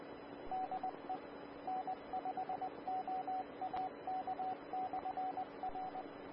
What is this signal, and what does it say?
Morse code, heard as a keyed tone, about 20 wpm, 750 Hz, with radio noise. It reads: DE N5OAK/R
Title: Austin, TX Amateur Repeaters